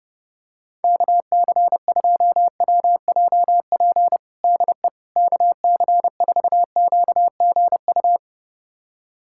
Morse code, heard as a keyed tone, about 30 wpm, 700 Hz. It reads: KC2WJP DE KC4QGU
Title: USA Callsign DE Callsign - 6 Characters - 30wpm